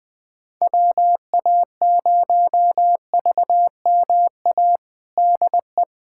Morse code, heard as a keyed tone, about 19 wpm, 700 Hz. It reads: WA0VMA DE
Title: USA Callsign DE Callsign 20wpm